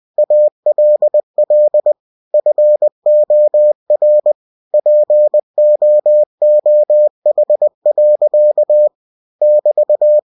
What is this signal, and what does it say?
Morse code, heard as a keyed tone, about 20 wpm, 600 Hz. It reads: ALL FOR POOH. <BT>
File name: cw-books / winnie-the-pooh / winnie-20wpm-page-0109